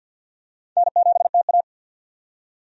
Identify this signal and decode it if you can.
Morse code, heard as a keyed tone, about 52 wpm, 700 Hz. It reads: N7TA
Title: USA Call Signs - 4 Characters - 50wpm